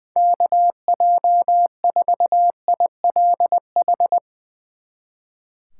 Morse code, heard as a keyed tone, about 21 wpm, 700 Hz. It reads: KJ4ILH
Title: Speed Racing - USA 2x3 Call Signs 20wpm